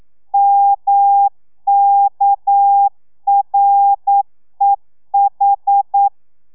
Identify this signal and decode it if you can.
Morse code, heard as a keyed tone, about 9 wpm, 800 Hz. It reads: MKREH
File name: telegrafia / i2tfj / Lez6 40